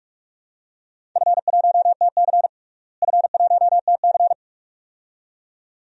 Morse code, heard as a keyed tone, about 43 wpm, 700 Hz. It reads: F1TC F1TC
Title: DXCC-100 Call Signs - 4 Characters - Repeated 2x - 45wpm